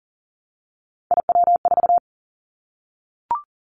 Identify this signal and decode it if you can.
Morse code, heard as a keyed tone, about 40 wpm, 700 Hz. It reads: IW4
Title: Callsign Prefixes for Top 25 DXCC - 40wpm